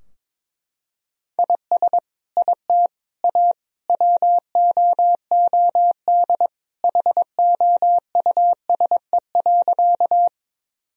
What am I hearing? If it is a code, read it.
Morse code, heard as a keyed tone, about 21 wpm, 700 Hz. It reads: IS IT A WOOD HOUSE.
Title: Sentences from Top 1000 Words - Review Part 2 - 22wpm